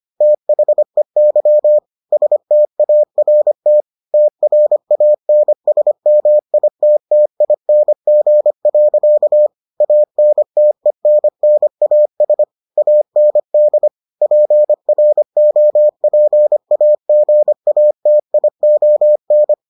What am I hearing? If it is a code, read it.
Morse code, heard as a keyed tone, about 25 wpm, 600 Hz. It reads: THEY START TRANSMITTING. ANTENNAS AND PROPAGATION